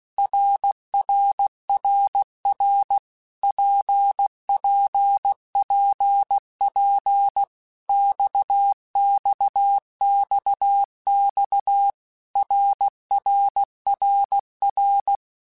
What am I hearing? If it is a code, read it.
Morse code, heard as a keyed tone, about 16 wpm, 800 Hz. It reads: RRRR PPPP XXXX RRRR
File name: Basic Letter Groups By7Groups / RKPX / RKPX@15